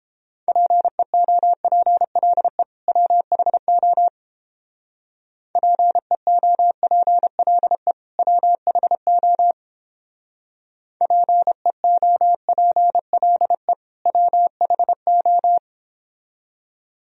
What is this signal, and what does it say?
Morse code, heard as a keyed tone, about 29 wpm, 700 Hz. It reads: PEOPLE WHO PEOPLE WHO PEOPLE WHO